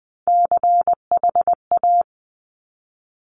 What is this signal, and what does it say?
Morse code, heard as a keyed tone, about 20 wpm, 700 Hz. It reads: CHA
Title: Top Three Letter Combinations in Words 20wpm